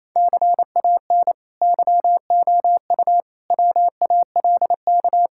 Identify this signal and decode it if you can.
Morse code, heard as a keyed tone, about 28 wpm, 700 Hz. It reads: CAN YOU WALK
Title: Sentences from Top 1000 Words - Review Part 6 - 28wpm